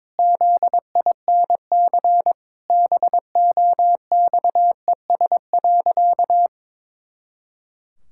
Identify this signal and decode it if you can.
Morse code, heard as a keyed tone, about 22 wpm, 700 Hz. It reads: ZINC BOXES.
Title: Pangram Challenge v2 - 22wpm